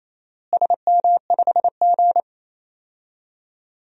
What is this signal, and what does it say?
Morse code, heard as a keyed tone, about 28 wpm, 700 Hz. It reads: SM5G
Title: DXCC-100 Call Signs - 4 Characters - Repeated 2x - 28wpm